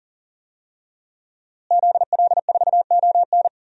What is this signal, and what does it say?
Morse code, heard as a keyed tone, about 40 wpm, 700 Hz. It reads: ZL4ON
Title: DXCC-100 Call Signs - 5 Characters - Repeated 3x - 40wpm